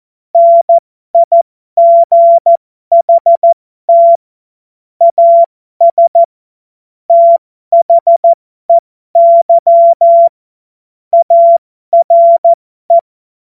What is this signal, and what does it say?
Morse code, heard as a keyed tone, about 14 wpm, spaced as for 12 wpm, 680 Hz. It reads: NIGHT AS THEY ARE
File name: Morsekurs / Witze